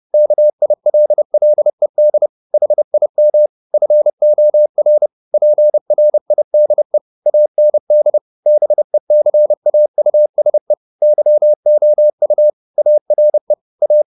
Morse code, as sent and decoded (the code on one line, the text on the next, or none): -.- .. .-.. .-.. -.. / .... .. -- / ..-. --- .-. / .--. .-. .. -.. / .- -. -.. / -... -.-. .- ..- ... / -.-- --- ..- / .- .-. / .-
KILLED HIM FOR PRIDE AND BECAUSE YOU ARE A